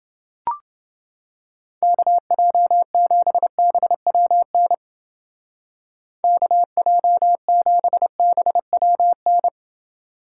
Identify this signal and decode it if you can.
Morse code, heard as a keyed tone, about 29 wpm, 700 Hz. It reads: KJ7BWN KJ7BWN